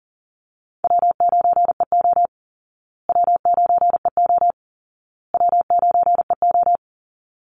Morse code, heard as a keyed tone, about 40 wpm, 700 Hz. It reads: W9EO W9EO W9EO
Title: Call Signs Repeated 2x - 40wpm